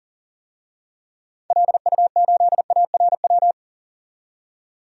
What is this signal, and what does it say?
Morse code, heard as a keyed tone, about 40 wpm, 700 Hz. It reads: LU8ARW
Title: DXCC-100 Call Signs - 6 Characters - 40wpm